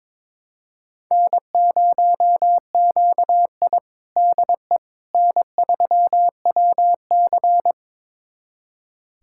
Morse code, heard as a keyed tone, about 21 wpm, 700 Hz. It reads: N0QI DE N3WC
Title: USA Callsign DE Callsign - 4 Characters - 22wpm